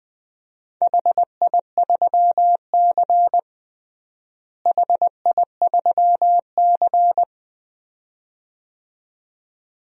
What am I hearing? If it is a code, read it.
Morse code, heard as a keyed tone, about 20 wpm, 700 Hz. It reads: HI3C HI3C
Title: DXCC-100 Call Signs - 4 Characters - Repeated 2x - 20wpm